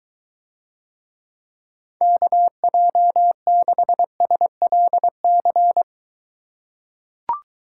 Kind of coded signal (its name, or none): Morse code